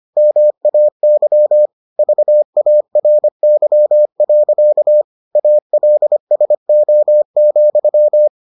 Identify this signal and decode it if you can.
Morse code, heard as a keyed tone, about 25 wpm, 600 Hz. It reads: MAY VARY. ALSO,